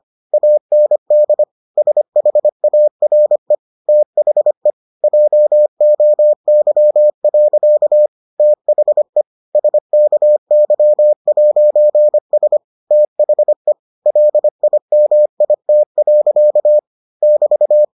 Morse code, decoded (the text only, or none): AND SHARE THE JOY. THE SKY'S THE LIMIT. <BT>